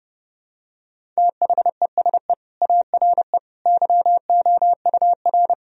Morse code, heard as a keyed tone, about 30 wpm, 700 Hz. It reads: THESE ARE YOUR
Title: Sentences from Top 1000 Words - Review Part 1 - 30wpm